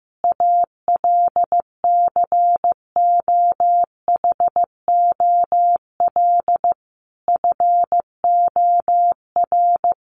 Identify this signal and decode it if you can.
Morse code, heard as a keyed tone, about 15 wpm, 700 Hz. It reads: ALCOHOL FOR
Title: Sentences from Top 1400 Words 15wpm